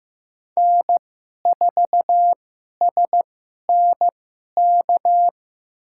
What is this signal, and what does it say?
Morse code, heard as a keyed tone, about 15 wpm, spaced as for 11 wpm, 700 Hz. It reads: N4SNK
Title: USA Call Signs - 5 Characters - 2x Character Spacing - 15wpm